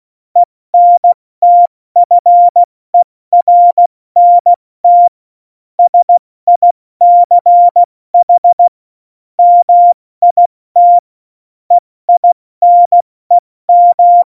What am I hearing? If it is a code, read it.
Morse code, heard as a keyed tone, about 16 wpm, 700 Hz. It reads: ENTFERNT SICH MIT EINEM
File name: Morsekurs / Witze / Witz 11 August 2020, 70 F80